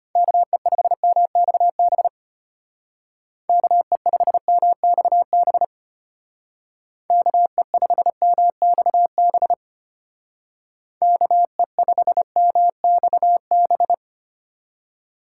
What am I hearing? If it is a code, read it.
Morse code, heard as a keyed tone, about 33 wpm, 700 Hz. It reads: KE5MXB KE5MXB KE5MXB KE5MXB